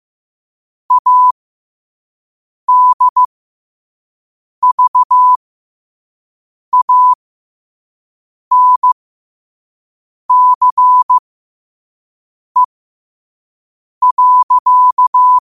Morse code, heard as a keyed tone, about 15 wpm, spaced as for 5 wpm, 1000 Hz. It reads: ADVANCE.
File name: qotdfiles / 5wpm